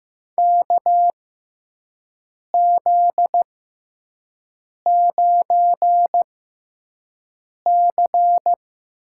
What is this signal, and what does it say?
Morse code, heard as a keyed tone, about 15 wpm, spaced as for 5 wpm, 700 Hz. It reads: KZ9C
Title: USA Call Signs - 4 Characters - 6x Character Spacing - 15wpm